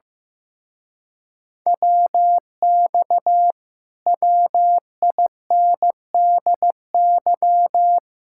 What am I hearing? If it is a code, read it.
Morse code, heard as a keyed tone, about 15 wpm, 700 Hz. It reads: WX WINDY
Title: Speed-Racing Common QSO Phrases Encore - 15wpm